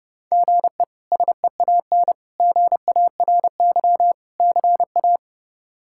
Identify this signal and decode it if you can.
Morse code, heard as a keyed tone, about 32 wpm, 700 Hz. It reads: GE SEAN GARY CA
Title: K1USN SST QSO Practice - 30wpm